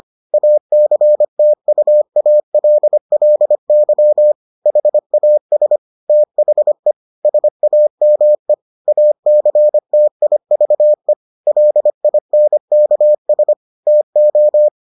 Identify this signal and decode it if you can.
Morse code, heard as a keyed tone, about 25 wpm, 600 Hz. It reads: ACTUALLY HAS THE SAME ACTIVE LINKS TO